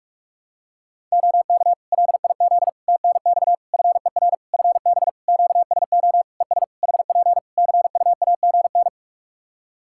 Morse code, heard as a keyed tone, about 46 wpm, 700 Hz. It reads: OK LIZ TNX FER FB QSO ES HP CUAGN
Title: K1USN SST QSO Practice 45wpm